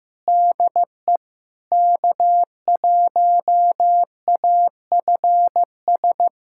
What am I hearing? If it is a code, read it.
Morse code, heard as a keyed tone, about 15 wpm, 690 Hz. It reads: DE K1AFS
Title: SOTA QSO Practice 15wpm